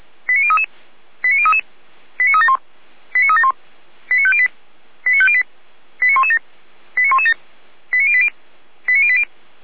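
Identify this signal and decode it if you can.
Five-tone selective call (ZVEI), heard as quick runs of five tones, tones of about 70 ms. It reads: 89033 89033 89371 89371 89508 89508 89117 89117 89080 89080
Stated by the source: ZVEI five-tone call 89120 received by SDR.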